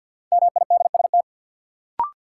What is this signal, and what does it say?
Morse code, heard as a keyed tone, about 50 wpm, 700 Hz. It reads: MIDST
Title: Sets of 4 Words - New Word Drill - 50wpm